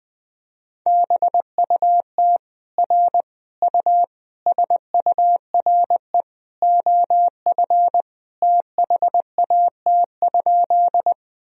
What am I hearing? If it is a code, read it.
Morse code, heard as a keyed tone, about 21 wpm, 700 Hz. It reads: BUT R U SURE OF THAT?
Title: Sentences from Top 300 Words and Abbreviations 20wpm